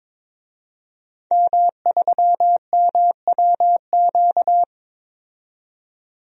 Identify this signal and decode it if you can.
Morse code, heard as a keyed tone, about 22 wpm, 700 Hz. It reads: M3MWQ
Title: DXCC-100 Call Signs - 5 Characters - 22wpm